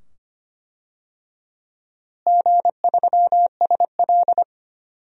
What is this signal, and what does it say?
Morse code, heard as a keyed tone, about 25 wpm, 700 Hz. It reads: G3SL
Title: DXCC-100 Call Signs - 4 Characters - Repeated 2x - 25wpm